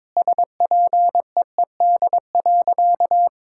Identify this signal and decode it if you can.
Morse code, heard as a keyed tone, about 21 wpm, 700 Hz. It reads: SPEED.
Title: Sentences from Top 1000 Words - Review Part 5 - 22wpm